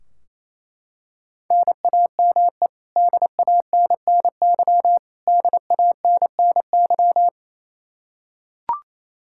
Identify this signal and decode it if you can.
Morse code, heard as a keyed tone, about 28 wpm, 700 Hz. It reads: NAME DANNY DANNY
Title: Practice QSO - Exchange 1 - Beginning - 28wpm